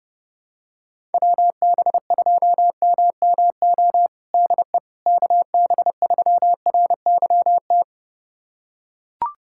Morse code, heard as a keyed tone, about 30 wpm, 700 Hz. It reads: WB2MMO DE KB3RYT
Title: USA Callsign DE Callsign - 6 Characters - 30wpm